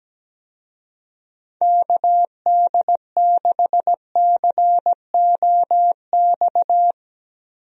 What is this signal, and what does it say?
Morse code, heard as a keyed tone, about 17 wpm, 700 Hz. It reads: KD6COX
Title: USA Call Signs - 6 Characters - Repeated 2x - 17wpm